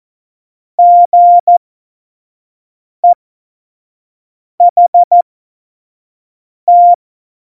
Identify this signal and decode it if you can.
Morse code, heard as a keyed tone, about 14 wpm, spaced as for 5 wpm, 700 Hz. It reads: GEHT
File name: Morsekurs / Witze / Witz 18 August 2020, 25 F70